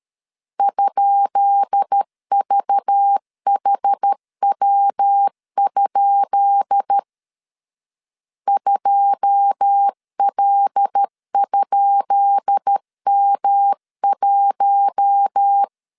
Morse code, heard as a keyed tone, about 13 wpm, 800 Hz. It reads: ?VHW? 2L?M1
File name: LEZIONI / 10 SIGNS / 10B INT 12wpm 2414